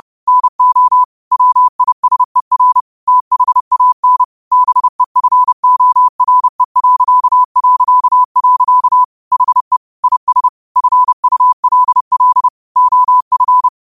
Morse code, heard as a keyed tone, about 30 wpm, 1000 Hz. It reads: NO WISER THAN BEFORE... HE IS FULL OF